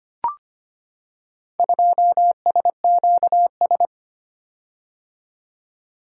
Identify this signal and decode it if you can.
Morse code, heard as a keyed tone, about 25 wpm, 700 Hz. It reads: 2SQS
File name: practice / 25wpm / Four Letter-Number Combinations - 25wpm